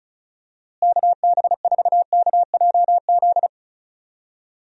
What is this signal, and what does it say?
Morse code, heard as a keyed tone, about 34 wpm, 700 Hz. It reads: KB4KJZ
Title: USA Call Signs - 6 Characters - Repeated 3x - 35wpm